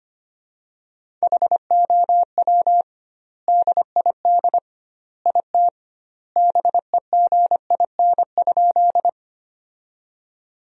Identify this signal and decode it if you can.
Morse code, heard as a keyed tone, about 25 wpm, 700 Hz. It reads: HOW DID IT BEGIN?
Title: 2x Word-Spacing - Sentences from Top 300 Words - 25wpm